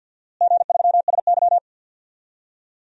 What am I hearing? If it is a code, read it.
Morse code, heard as a keyed tone, about 47 wpm, 700 Hz. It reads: G3SY
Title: DXCC-100 Call Signs - 4 Characters - Repeated 2x - 50wpm